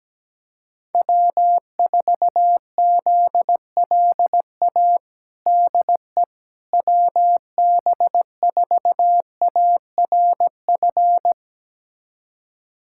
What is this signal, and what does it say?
Morse code, heard as a keyed tone, about 17 wpm, 700 Hz. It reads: W4ZLA DE WB4ARF
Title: USA Callsign DE Callsign 17wpm